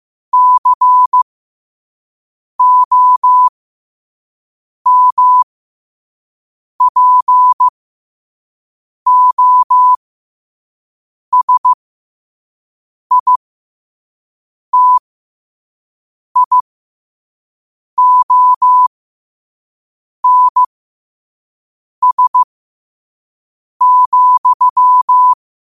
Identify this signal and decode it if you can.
Morse code, heard as a keyed tone, about 15 wpm, spaced as for 5 wpm, 1000 Hz. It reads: COMPOSITIONS,
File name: qotdfiles / 5wpm